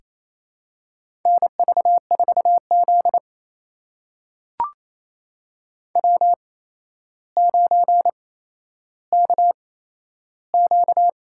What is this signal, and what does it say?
Morse code, heard as a keyed tone, about 28 wpm, spaced as for 12 wpm, 700 Hz. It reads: NV4Z W9KQ
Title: USA Call Signs - 4 Characters - 8x Character Spacing - 28wpm